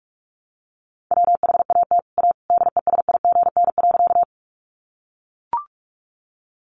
Morse code, heard as a keyed tone, about 44 wpm, 700 Hz. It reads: WHAT A DESIGN.
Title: Sentences from Top 800 Words 45wpm